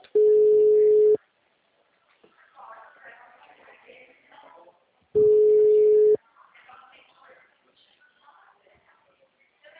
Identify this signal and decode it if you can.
Ringback tone (Europe, 425 Hz), with speech.
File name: out- -2004- - - .259